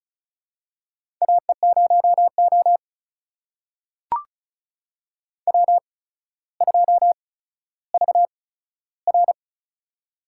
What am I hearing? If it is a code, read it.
Morse code, heard as a keyed tone, about 34 wpm, spaced as for 15 wpm, 700 Hz. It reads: AE0O W2VR
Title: USA Call Signs - 4 Characters - 8x Character Spacing - 35wpm